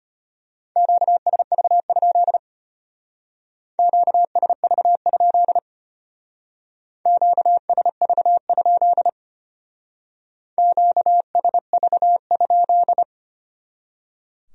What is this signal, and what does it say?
Morse code, heard as a keyed tone, about 32 wpm, 700 Hz. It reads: QSV? QSV? QSV? QSV?